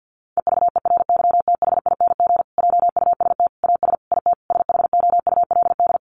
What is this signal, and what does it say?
Morse code, heard as a keyed tone, about 50 wpm, 700 Hz. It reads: EVERYTHING JUST AS IT SHOULD
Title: Sentences from Top 700 Words 50wpm